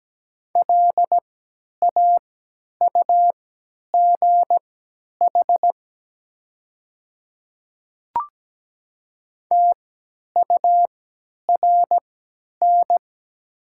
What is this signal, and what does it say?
Morse code, heard as a keyed tone, about 17 wpm, spaced as for 10 wpm, 700 Hz. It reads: LAUGH TURN
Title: 3x Character Spacing - Top 500 Words - 17wpm